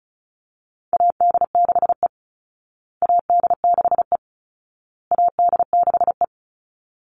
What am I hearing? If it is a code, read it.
Morse code, heard as a keyed tone, about 34 wpm, 700 Hz. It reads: AD6E AD6E AD6E
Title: Call Signs Repeated 2x - 35wpm